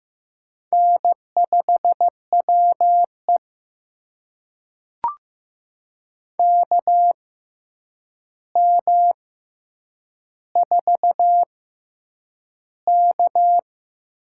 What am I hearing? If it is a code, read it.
Morse code, heard as a keyed tone, about 15 wpm, spaced as for 8 wpm, 700 Hz. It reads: N5WE KM4K
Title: USA Call Signs - 6x Character Spacing - 15wpm